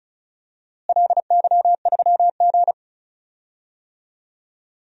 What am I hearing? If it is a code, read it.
Morse code, heard as a keyed tone, about 36 wpm, 690 Hz. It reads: LY3G
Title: DXCC-100 Call Signs - 4 Characters - Repeated 2x - 35wpm